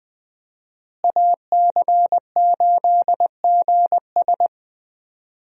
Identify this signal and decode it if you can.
Morse code, heard as a keyed tone, about 21 wpm, 700 Hz. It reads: AC8GS
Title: USA Call Signs - 5 Characters - Repeated 2x - 20wpm